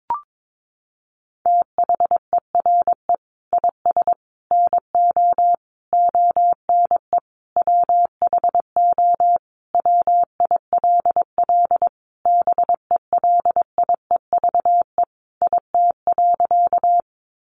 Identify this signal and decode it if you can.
Morse code, heard as a keyed tone, about 22 wpm, 700 Hz. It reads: THERE IS NO ONE WHO WILL BELIEVE IT.
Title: Sentences from Top 1000 Words 22wpm